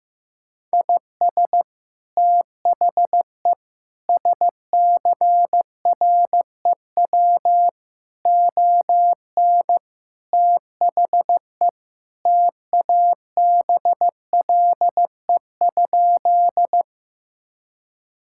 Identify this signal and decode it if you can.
Morse code, heard as a keyed tone, about 15 wpm, 700 Hz. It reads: IS THE SCREW ON THE TABLE?
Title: Sentences from Top 1600 Words 15wpm